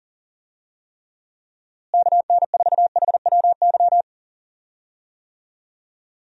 Morse code, heard as a keyed tone, about 40 wpm, 700 Hz. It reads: KN4HWY
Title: USA Call Signs - 6 Characters - Repeated 3x - 40wpm